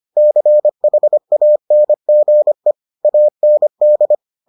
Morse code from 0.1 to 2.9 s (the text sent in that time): CHANGE